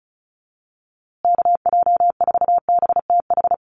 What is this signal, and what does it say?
Morse code, heard as a keyed tone, about 34 wpm, 700 Hz. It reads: KJ4BTH
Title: USA Call Signs Repeated 3x 35wpm